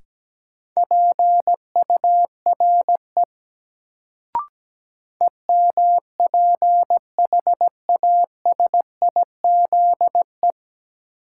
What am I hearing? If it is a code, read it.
Morse code, heard as a keyed tone, about 17 wpm, 700 Hz. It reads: PURE EMPHASIZE
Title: Sentences from Top 1500 Words - New Word Drill Alt - 17wpm